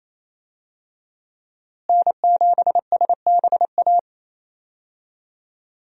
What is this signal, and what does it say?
Morse code, heard as a keyed tone, about 27 wpm, 700 Hz. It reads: N7SBA
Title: USA Call Signs - 5 Characters - Repeated 3x - 28wpm